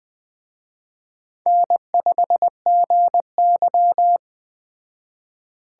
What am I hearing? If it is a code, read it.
Morse code, heard as a keyed tone, about 21 wpm, 700 Hz. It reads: N5GY